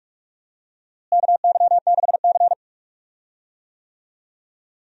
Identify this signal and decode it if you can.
Morse code, heard as a keyed tone, about 46 wpm, 700 Hz. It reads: KY6C
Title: USA Call Signs - 4 Characters - Repeated 2x - 45wpm